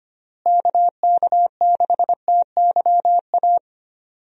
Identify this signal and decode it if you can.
Morse code, heard as a keyed tone, about 25 wpm, 700 Hz. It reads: KK6TYA
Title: USA Call Signs - 6 Characters - Repeated 2x - 25wpm